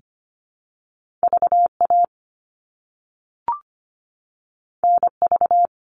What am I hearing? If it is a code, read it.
Morse code, heard as a keyed tone, about 25 wpm, 700 Hz. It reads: VA NV
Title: All Cut Numbers - 2-Digit 25wpm